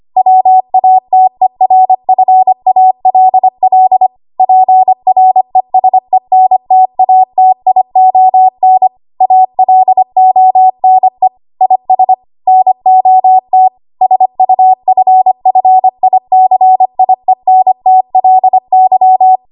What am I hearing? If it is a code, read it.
Morse code, heard as a keyed tone, about 25 wpm, 750 Hz. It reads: WATERFALL PRESENTATION ALONE IS NOT SUFFICIENTLY
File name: Archive / 25 WPM / 25WPM